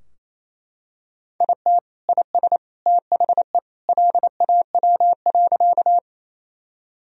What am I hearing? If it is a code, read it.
Morse code, heard as a keyed tone, about 28 wpm, 700 Hz. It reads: IT IS THE LAW.
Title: Sentences from Top 1000 Words - Review Part 3 - 28wpm